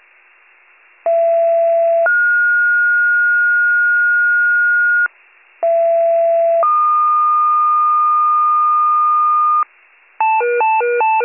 Two-tone page, heard as a one-second tone, then a three-second tone, about 670 then 1430 Hz, about 670 then 1220 Hz, with noise. It is heard from 1.1 to 9.6 s.